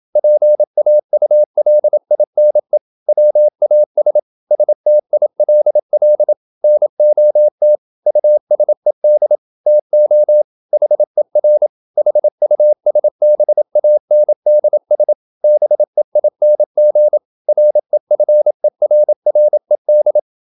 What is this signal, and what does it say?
Morse code, heard as a keyed tone, about 27 wpm, 600 Hz. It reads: PAULINE WAS STILL NOT USED TO HER HUSBANDS BEING REFERRED
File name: Never 0027wpm